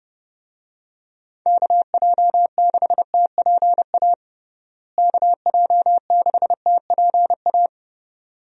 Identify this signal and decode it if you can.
Morse code, heard as a keyed tone, about 30 wpm, 700 Hz. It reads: KJ6TPA KJ6TPA
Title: USA Call Signs - 6 Characters - Repeated 2x - 30wpm